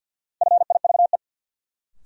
Morse code, heard as a keyed tone, about 53 wpm, 700 Hz. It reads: FIVE
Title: Sentences from Top 400 Words - New Word Drill - 50wpm